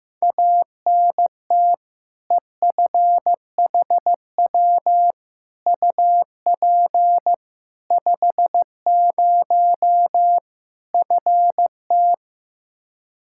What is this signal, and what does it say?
Morse code, heard as a keyed tone, about 15 wpm, 700 Hz. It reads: ANT EFHW UP 50 FT
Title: Practice QSO - Exchange 2 - Info - 15wpm